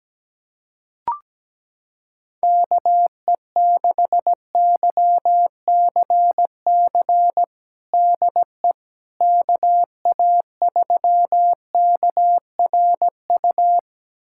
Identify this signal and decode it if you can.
Morse code, heard as a keyed tone, about 17 wpm, 700 Hz. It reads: KE6YCC DE KA3KRU
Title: USA Callsign DE Callsign 17wpm